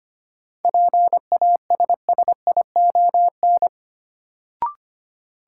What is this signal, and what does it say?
Morse code, heard as a keyed tone, about 25 wpm, 700 Hz. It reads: PASSION
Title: Sentences from Top 1300 Words - New Word Drill Alt - 25wpm